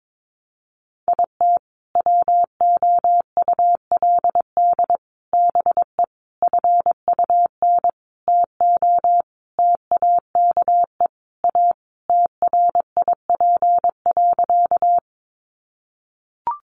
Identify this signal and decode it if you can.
Morse code, heard as a keyed tone, about 22 wpm, 700 Hz. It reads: IT WOULD BE FUN TO TAKE A TRIP.
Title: Sentences from Top 1300 Words 22wpm